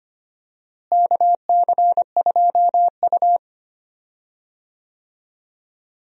Morse code, heard as a keyed tone, about 25 wpm, 700 Hz. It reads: KC2U
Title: USA Call Signs - 4 Characters - Repeated 2x - 25wpm